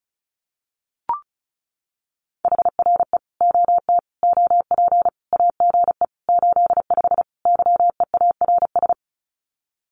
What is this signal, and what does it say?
Morse code, heard as a keyed tone, about 34 wpm, 700 Hz. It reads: HRE OT OP AGE 85 YEARS
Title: Common QSO Phrases 35wpm